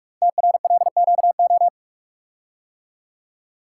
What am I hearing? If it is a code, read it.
Morse code, heard as a keyed tone, about 44 wpm, 700 Hz. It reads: TRLQO
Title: Code Groups - 5 Letter:Numbers - 45wpm